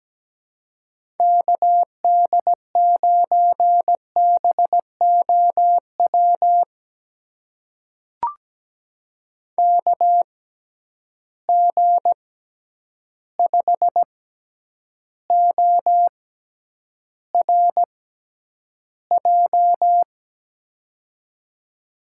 Morse code, heard as a keyed tone, about 17 wpm, spaced as for 9 wpm, 700 Hz. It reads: KD9BOW KG5ORJ
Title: USA Call Signs - 6 Characters - 6x Character Spacing - 17wpm